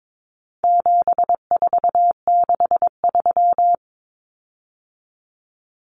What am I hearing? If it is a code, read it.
Morse code, heard as a keyed tone, about 22 wpm, 700 Hz. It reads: 7463
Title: Common Cut Numbers - 4-Digit 22wpm